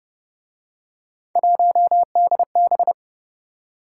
Morse code, heard as a keyed tone, about 30 wpm, 700 Hz. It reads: 1DB
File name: Speed-Racing - Three Letter-Number Combinations - 25wpm